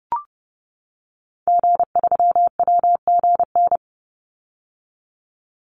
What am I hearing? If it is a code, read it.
Morse code, heard as a keyed tone, about 30 wpm, 700 Hz. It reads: G3WGN
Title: Call Signs - 30wpm